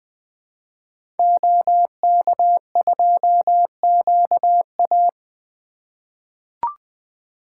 Morse code, heard as a keyed tone, about 20 wpm, 700 Hz. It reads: OK2QA
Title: 2023 SOTA Activator Call Signs - 20wpm